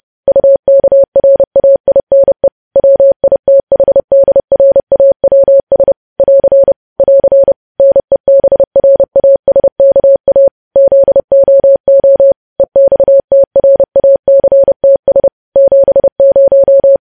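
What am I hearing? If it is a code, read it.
Morse code, heard as a keyed tone, about 30 wpm, 550 Hz. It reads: UKRAINE WITHDRAWS <AR> <AR> NEBRASKA ZOO EXTRACTS 70
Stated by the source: Morse code news headlines sent at 30 wpm.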